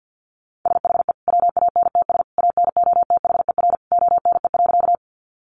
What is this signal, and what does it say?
Morse code, heard as a keyed tone, about 50 wpm, 700 Hz. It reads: SHE WANTS ANOTHER ONE.
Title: Easy Phrases 50wpm